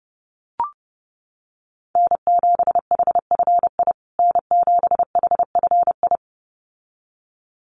Morse code, heard as a keyed tone, about 30 wpm, 700 Hz. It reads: N7HFI N7HFI
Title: USA Call Signs Repeated 2x 30wpm